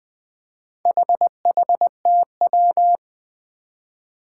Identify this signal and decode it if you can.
Morse code, heard as a keyed tone, about 20 wpm, 700 Hz. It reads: HHTW
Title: Four Challenging Letter-Number Combinations 20wpm